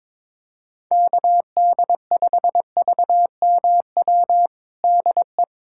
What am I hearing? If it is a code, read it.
Morse code, heard as a keyed tone, about 22 wpm, 700 Hz. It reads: KD5VMW DE
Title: USA Callsign DE Callsign - 6 Characters - 22wpm